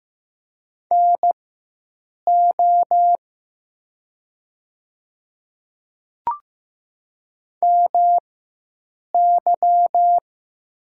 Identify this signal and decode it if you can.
Morse code, heard as a keyed tone, about 15 wpm, spaced as for 7 wpm, 700 Hz. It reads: NO MY
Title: 4x Character Spacing - Top 300 Words - 15wpm